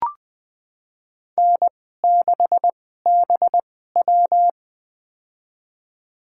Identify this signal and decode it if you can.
Morse code, heard as a keyed tone, about 20 wpm, spaced as for 15 wpm, 700 Hz. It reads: N6BW
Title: USA Call Signs - 4 Characters - 2x Character Spacing - 20wpm